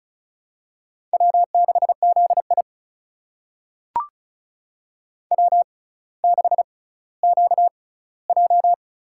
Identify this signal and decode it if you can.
Morse code, heard as a keyed tone, about 36 wpm, spaced as for 18 wpm, 700 Hz. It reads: W6ZI W6QJ
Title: USA Call Signs - 4 Characters - 6x Character Spacing - 35wpm